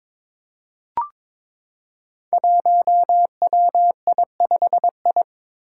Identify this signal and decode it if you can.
Morse code, heard as a keyed tone, about 23 wpm, 700 Hz. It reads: 1WI5I
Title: Five Challenging Letter-Number Combinations 22wpm